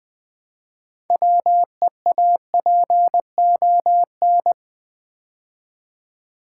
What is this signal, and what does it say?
Morse code, heard as a keyed tone, about 20 wpm, 700 Hz. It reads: WEAPON
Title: Build A Word - 2 Letters At A Time - 6-Letter Words - 20wpm